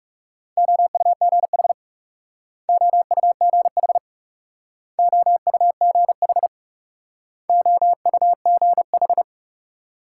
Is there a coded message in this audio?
Yes: Morse code from 0.6 to 9.2 s, heard as a keyed tone, about 42 wpm, 700 Hz. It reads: OUGH OUGH OUGH OUGH